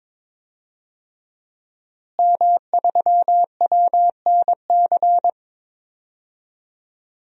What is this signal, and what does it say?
Morse code, heard as a keyed tone, about 21 wpm, 700 Hz. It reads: M3WNC
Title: DXCC-100 Call Signs - 5 Characters - Repeated 2x - 22wpm